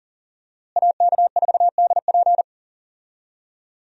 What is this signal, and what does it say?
Morse code, heard as a keyed tone, about 40 wpm, 700 Hz. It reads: AK4DP
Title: USA Call Signs - 5 Characters - Repeated 3x - 40wpm